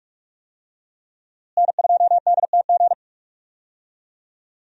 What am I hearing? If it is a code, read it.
Morse code, heard as a keyed tone, about 46 wpm, 700 Hz. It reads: N2DTG
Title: USA Call Signs - 5 Characters - Repeated 3x - 45wpm